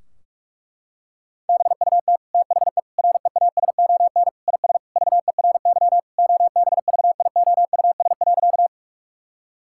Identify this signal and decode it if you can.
Morse code, heard as a keyed tone, about 46 wpm, 700 Hz. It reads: BUT THE REASON IS VERY OBVIOUS.
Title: Sentences from Top 1600 Words 45wpm